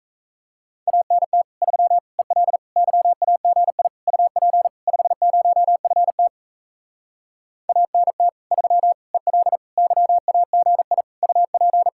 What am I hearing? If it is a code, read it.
Morse code, heard as a keyed tone, about 40 wpm, 700 Hz. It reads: ANT 3 EL YAGI UP 50FT ANT 3 EL YAGI UP